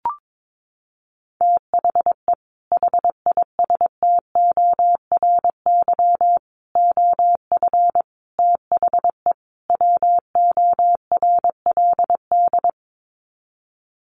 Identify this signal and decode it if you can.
Morse code, heard as a keyed tone, about 22 wpm, 700 Hz. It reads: THE HISTORY OF THE WORLD
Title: Sets of 5 Words Encore2 22wpm